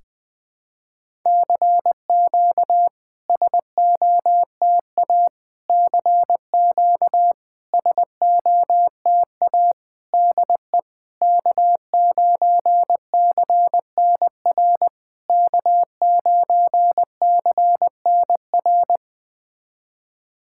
Morse code, heard as a keyed tone, about 20 wpm, 690 Hz. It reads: CQ SOTA CQ SOTA DE K9CNR K9CNR
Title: SOTA QSO Practice 20wpm